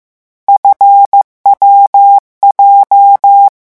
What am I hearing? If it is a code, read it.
Morse code, heard as a keyed tone, about 15 wpm, 800 Hz. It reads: FWJ
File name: morse mp3 / lecon7-vitesse 14